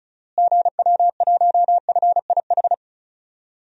Morse code, heard as a keyed tone, about 36 wpm, 700 Hz. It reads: GW1FIH
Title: DXCC-100 Call Signs - 6 Characters - Repeated 3x - 35wpm